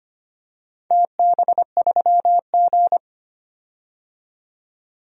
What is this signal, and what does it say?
Morse code, heard as a keyed tone, about 25 wpm, 700 Hz. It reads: TB3G
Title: DXCC-100 Call Signs - 4 Characters - 25wpm